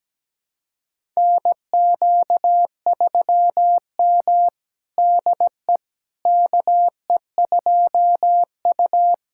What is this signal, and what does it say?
Morse code, heard as a keyed tone, about 17 wpm, 700 Hz. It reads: NQ3M DE KE2U
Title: USA Callsign DE Callsign - 4 Characters - 17wpm